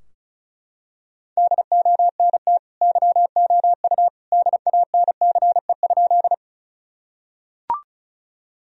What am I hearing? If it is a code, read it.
Morse code, heard as a keyed tone, about 36 wpm, 700 Hz. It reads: DONT YOU DANCE?
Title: Sentences from Top 1000 Words - Review Part 4 - 35wpm